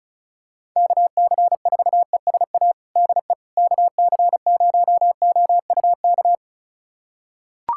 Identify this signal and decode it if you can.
Morse code, heard as a keyed tone, about 36 wpm, 700 Hz. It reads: KC4ESA DE KC0OUK
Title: USA Callsign DE Callsign - 6 Characters - 35wpm